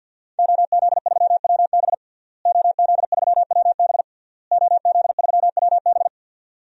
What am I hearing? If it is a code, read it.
Morse code, heard as a keyed tone, about 51 wpm, 700 Hz. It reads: OZ3WB OZ3WB OZ3WB
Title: DXCC-100 Call Signs - 5 Characters - Repeated 3x - 50wpm